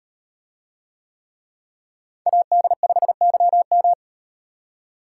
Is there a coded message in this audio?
Yes: Morse code from 2.3 to 3.9 s, heard as a keyed tone, about 38 wpm, 700 Hz. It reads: AD5YM